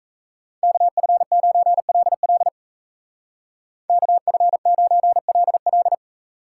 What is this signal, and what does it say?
Morse code, heard as a keyed tone, about 35 wpm, 700 Hz. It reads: KF9LL KF9LL
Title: Speed Racing - USA 2x2 Call Signs 28wpm